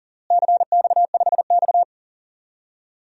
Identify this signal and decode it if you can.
Morse code, heard as a keyed tone, about 40 wpm, 700 Hz. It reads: CX5X
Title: DXCC-100 Call Signs - 4 Characters - Repeated 2x - 40wpm